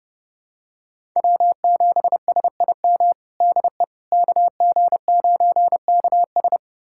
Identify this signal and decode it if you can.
Morse code, heard as a keyed tone, about 30 wpm, 700 Hz. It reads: W7SIM DE KG9KS
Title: USA Callsign DE Callsign - 5 Characters - 30wpm